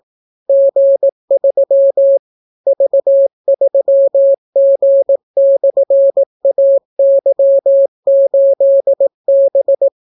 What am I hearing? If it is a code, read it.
Morse code, heard as a keyed tone, about 18 wpm, 550 Hz. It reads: G3 V3G/AY8B